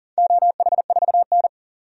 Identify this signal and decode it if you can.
Morse code, heard as a keyed tone, about 43 wpm, 700 Hz. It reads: OH4N
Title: DXCC-100 Call Signs - 4 Characters - 40wpm